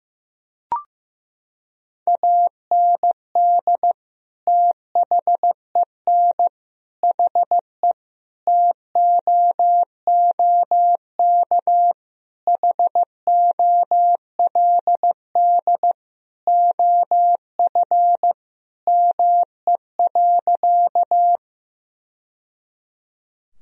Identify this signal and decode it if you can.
Morse code, heard as a keyed tone, about 15 wpm, 700 Hz. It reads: AND THEN HE TOOK HOLD OF ME.
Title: Sentences from Top 1000 Words - Review Part 2 - 15wpm